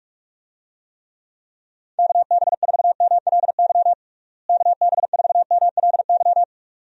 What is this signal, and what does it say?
Morse code, heard as a keyed tone, about 46 wpm, 700 Hz. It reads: KB4MLY KB4MLY
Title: USA Call Signs - 6 Characters - Repeated 2x - 45wpm